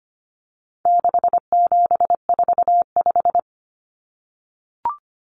Morse code, heard as a keyed tone, about 25 wpm, 700 Hz. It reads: 6745
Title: Numbers - 4-Digit 25wpm